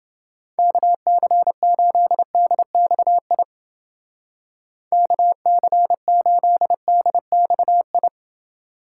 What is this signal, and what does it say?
Morse code, heard as a keyed tone, about 28 wpm, 700 Hz. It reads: KC8DXI KC8DXI